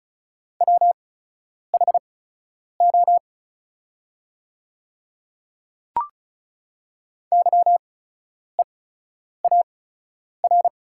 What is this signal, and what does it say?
Morse code, heard as a keyed tone, about 34 wpm, spaced as for 10 wpm, 700 Hz. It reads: WHO YEAR
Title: 8x Character Spacing - Top 300 Words - 35wpm